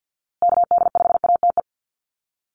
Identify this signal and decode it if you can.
Morse code, heard as a keyed tone, about 48 wpm, 700 Hz. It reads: KD5ATE
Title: USA Call Signs Repeated 2x 50wpm